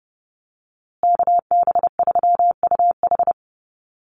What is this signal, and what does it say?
Morse code, heard as a keyed tone, about 30 wpm, 700 Hz. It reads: KB3UH
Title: USA Call Signs Repeated 3x 30wpm